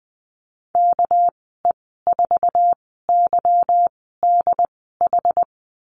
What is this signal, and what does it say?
Morse code, heard as a keyed tone, about 20 wpm, spaced as for 14 wpm, 700 Hz. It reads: KE4YDH
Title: 213 Lesson 34g - Number 8 - Call Signs 20wpm